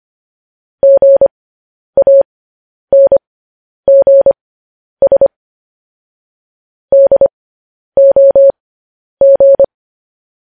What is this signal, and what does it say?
Morse code, heard as a keyed tone, about 25 wpm, spaced as for 10 wpm, 550 Hz. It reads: GANGS DOG
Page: News Headlines in Morse Code at 10 WPM